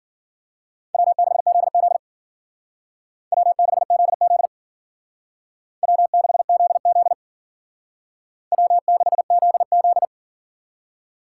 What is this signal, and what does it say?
Morse code, heard as a keyed tone, about 48 wpm, 700 Hz. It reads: W6ZZ W6ZZ W6ZZ W6ZZ
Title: Speed Racing - USA 1x2 Call Signs 40wpm